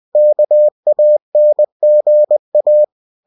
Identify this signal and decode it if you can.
Morse code, heard as a keyed tone, about 20 wpm, 600 Hz. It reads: KANGA
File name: cw-books / winnie-the-pooh / winnie-20wpm-page-0066